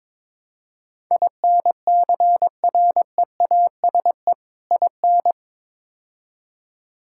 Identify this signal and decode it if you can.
Morse code, heard as a keyed tone, about 22 wpm, 700 Hz. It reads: INCREASE IN
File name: practice / 22wpm / Build A Phrase - 5-Word Phrases - 22wpm